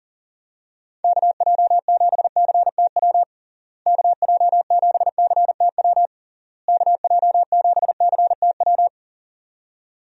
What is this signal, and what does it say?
Morse code, heard as a keyed tone, about 40 wpm, 700 Hz. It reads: KJ7CTW KJ7CTW KJ7CTW
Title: USA Call Signs - 6 Characters - Repeated 3x - 40wpm